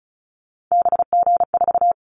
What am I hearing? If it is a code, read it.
Morse code, heard as a keyed tone, about 34 wpm, 700 Hz. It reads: BG4
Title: Callsign Prefixes for Top 100 DXCC - 35wpm